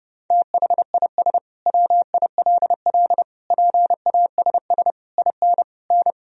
Morse code, heard as a keyed tone, about 30 wpm, 700 Hz. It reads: THIS WILL PASS IN N
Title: Sentences from Top 400 Words and Abbreviations 30wpm